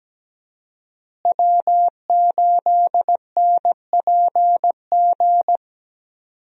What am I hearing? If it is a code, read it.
Morse code, heard as a keyed tone, about 17 wpm, 700 Hz. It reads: W8NPG
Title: USA Call Signs - 5 Characters - Repeated 3x - 17wpm